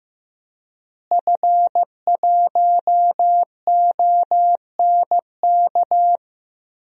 Morse code, heard as a keyed tone, about 15 wpm, 700 Hz. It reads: F1ONK
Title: DXCC-100 Call Signs - 5 Characters - Repeated 2x - 15wpm